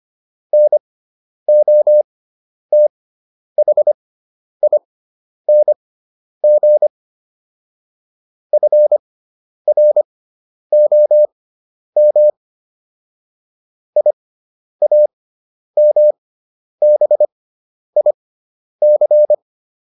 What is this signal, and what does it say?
Morse code, heard as a keyed tone, about 25 wpm, spaced as for 10 wpm, 600 Hz. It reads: NOTHING FROM IAMBIC